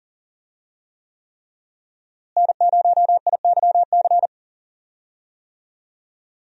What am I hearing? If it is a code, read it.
Morse code, heard as a keyed tone, about 40 wpm, 700 Hz. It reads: N0IYC
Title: USA Call Signs - 5 Characters - Repeated 3x - 40wpm